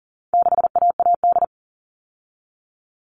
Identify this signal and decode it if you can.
Morse code, heard as a keyed tone, about 40 wpm, 700 Hz. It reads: 6AAD
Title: All Cut Numbers - 4-Digit 40wpm